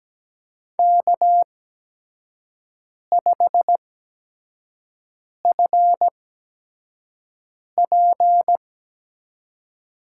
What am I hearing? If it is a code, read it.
Morse code, heard as a keyed tone, about 17 wpm, spaced as for 5 wpm, 700 Hz. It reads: K5FP
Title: USA Call Signs - 8x Character Spacing - 17wpm